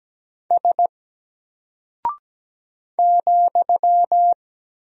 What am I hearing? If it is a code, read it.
Morse code, heard as a keyed tone, about 17 wpm, 700 Hz. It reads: S ,
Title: Single Character (Letter:Number:Punctuation:Prosign) 17wpm